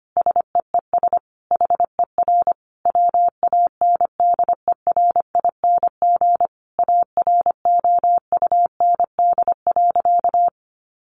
Morse code, heard as a keyed tone, about 25 wpm, 700 Hz. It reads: SEES HER WANDERING AROUND.
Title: Additional Easy Phrases 25wpm